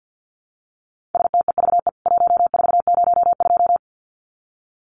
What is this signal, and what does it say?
Morse code, heard as a keyed tone, about 50 wpm, 700 Hz. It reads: STEVE 1402
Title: CWT Exchange 50wpm